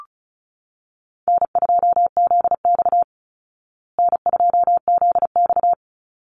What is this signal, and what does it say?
Morse code, heard as a keyed tone, about 34 wpm, 700 Hz. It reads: N2ZX N2ZX
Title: Easy Call Signs Repeated 3x - 35wpm